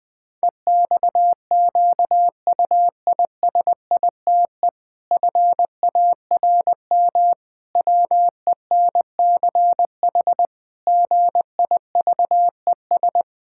Morse code, heard as a keyed tone, about 20 wpm, 700 Hz. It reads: EXQUISITE FARM WENCH GIVES
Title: Pangram Challenge v2 - 20wpm